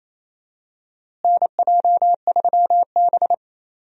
Morse code, heard as a keyed tone, about 28 wpm, 700 Hz. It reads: NJ3B
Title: USA Call Signs - 4 Characters - Repeated 3x - 28wpm